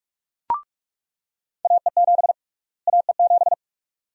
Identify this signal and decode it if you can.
Morse code, heard as a keyed tone, about 41 wpm, 700 Hz. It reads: AE7 AE7
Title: USA Call Sign Prefixes - Repeated 2x - 45wpm